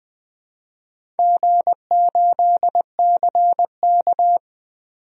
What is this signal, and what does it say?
Morse code, heard as a keyed tone, about 20 wpm, 700 Hz. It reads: G8CK
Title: DXCC-100 Call Signs - 4 Characters - 20wpm